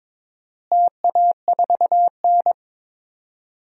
Morse code, heard as a keyed tone, about 22 wpm, 700 Hz. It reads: TA4N
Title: DXCC-100 Call Signs - 4 Characters - Repeated 3x - 22wpm